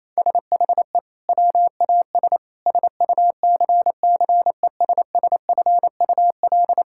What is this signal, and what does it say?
Morse code, heard as a keyed tone, about 28 wpm, 700 Hz. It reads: SHE WAS SUCCESSFUL
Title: Sentences from Top 1000 Words - Review Part 6 - 28wpm